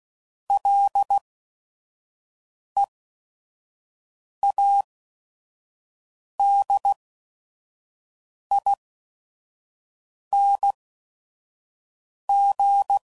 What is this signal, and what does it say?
Morse code, heard as a keyed tone, about 16 wpm, spaced as for 5 wpm, 800 Hz. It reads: LEADING
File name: Prefixes / FLs & THs / Single Fs Ls